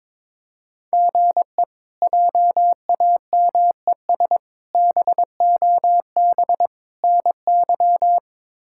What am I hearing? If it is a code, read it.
Morse code, heard as a keyed tone, about 21 wpm, 700 Hz. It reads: GE JAMES BOB NY
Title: K1USN SST QSO Practice - 22wpm